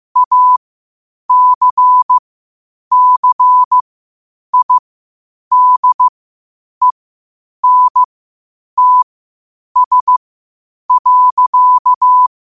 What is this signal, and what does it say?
Morse code, heard as a keyed tone, about 15 wpm, spaced as for 8 wpm, 1000 Hz. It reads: ACCIDENTS.
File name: qotdfiles / 8wpm